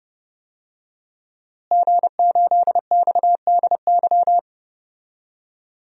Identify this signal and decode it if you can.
Morse code, heard as a keyed tone, about 30 wpm, 700 Hz. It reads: G8XDY
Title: DXCC-100 Call Signs - 5 Characters - Repeated 3x - 30wpm